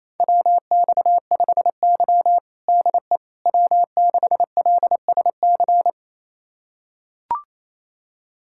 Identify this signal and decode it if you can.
Morse code, heard as a keyed tone, about 27 wpm, 700 Hz. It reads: WX5Y DE W6LSC
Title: USA Callsign DE Callsign 28wpm